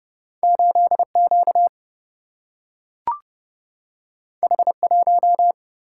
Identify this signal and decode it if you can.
Morse code, heard as a keyed tone, about 30 wpm, 700 Hz. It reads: 8 Q H 1
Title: Single-Letter Single-Number 30wpm